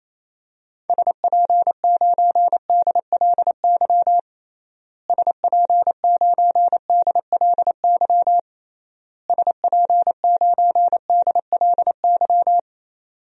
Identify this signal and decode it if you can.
Morse code, heard as a keyed tone, about 28 wpm, 700 Hz. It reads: SP9DLY SP9DLY SP9DLY
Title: DXCC-100 Call Signs - 6 Characters - Repeated 3x - 28wpm